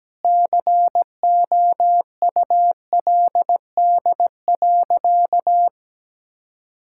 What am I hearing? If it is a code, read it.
Morse code, heard as a keyed tone, about 17 wpm, 700 Hz. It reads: COULD.
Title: Sentences from Top 1600 Words 17wpm